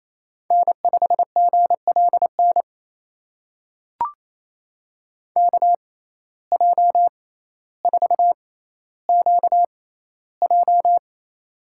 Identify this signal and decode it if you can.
Morse code, heard as a keyed tone, about 28 wpm, spaced as for 14 wpm, 700 Hz. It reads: N5GLN KJ4QJ
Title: USA Call Signs - 5 Characters - 6x Character Spacing - 28wpm